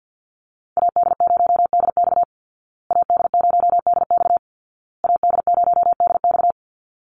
Morse code, heard as a keyed tone, about 50 wpm, 700 Hz. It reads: AD0DX AD0DX AD0DX
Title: Call Signs Repeated 2x - 50wpm